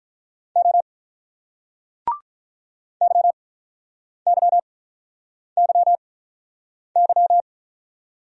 Morse code, heard as a keyed tone, about 48 wpm, spaced as for 9 wpm, 700 Hz. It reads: O YYYY